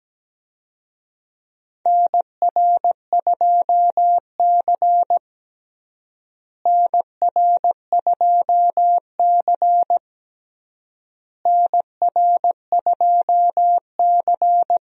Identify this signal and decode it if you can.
Morse code, heard as a keyed tone, about 17 wpm, 700 Hz. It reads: NR2C NR2C NR2C
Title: USA Call Signs - 4 Characters - Repeated 3x - 17wpm